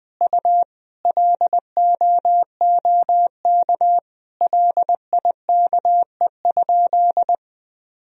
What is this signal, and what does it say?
Morse code, heard as a keyed tone, about 20 wpm, 700 Hz. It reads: U LOOK LIKE?
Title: Sentences from Top 100 Words and Abbreviations 20wpm